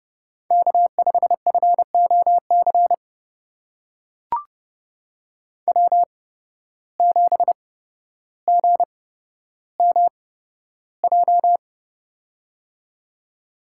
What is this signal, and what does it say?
Morse code, heard as a keyed tone, about 30 wpm, spaced as for 13 wpm, 700 Hz. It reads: K5FOC W7GMJ
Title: USA Call Signs - 5 Characters - 8x Character Spacing - 30wpm